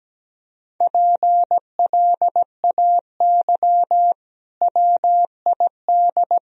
Morse code, heard as a keyed tone, about 17 wpm, 700 Hz. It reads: PLAY WID
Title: Sentences from Top 200 Words and Abbreviations 17wpm